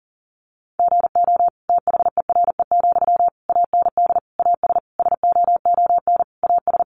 Morse code, heard as a keyed tone, about 40 wpm, 700 Hz. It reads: GO THERE, AND AS SOON AS
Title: Sentences from Top 1300 Words 40wpm